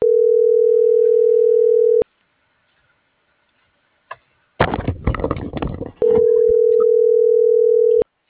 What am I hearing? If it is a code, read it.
Ringback tone (North America), 440 + 480 Hz.